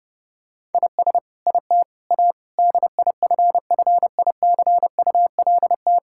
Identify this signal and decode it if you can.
Morse code, heard as a keyed tone, about 30 wpm, 700 Hz. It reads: IS IT A DIFFICULT
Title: Sentences from Top 1000 Words - Review Part 6 - 30wpm